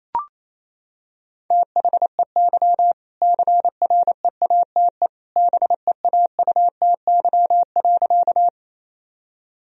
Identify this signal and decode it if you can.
Morse code, heard as a keyed tone, about 28 wpm, 700 Hz. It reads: THEY CREATE BEAUTY.
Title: Sentences from Top 1000 Words - Review Part 3 - 28wpm